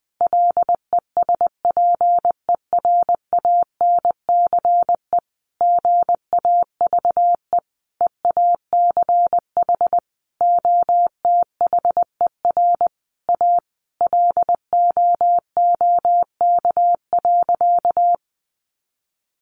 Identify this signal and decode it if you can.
Morse code, heard as a keyed tone, about 20 wpm, 700 Hz. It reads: LESPERANCE GAVE EACH OTHER A LOOK.
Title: A Sound Of Thunder 20wpm